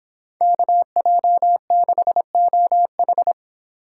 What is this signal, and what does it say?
Morse code, heard as a keyed tone, about 26 wpm, 700 Hz. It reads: KJ6OH